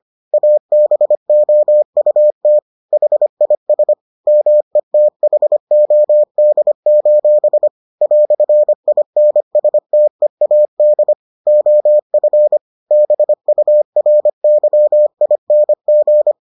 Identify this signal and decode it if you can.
Morse code, heard as a keyed tone, about 25 wpm, 600 Hz. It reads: ABOUT HIS METHOD: "INSTEAD OF BURYING